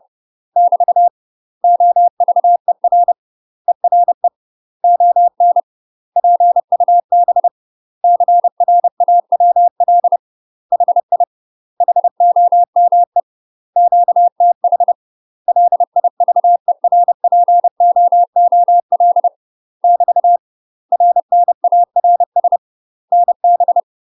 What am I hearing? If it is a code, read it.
Morse code, heard as a keyed tone, about 30 wpm, 700 Hz. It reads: <BT> OVER ERE ON PUB CRAWL HI HOME QTH LIVERPOOL <BT> RNARS NB
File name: UK MASTER QSOs TRAINING / 30WPM / UK MASTER QSOs 0020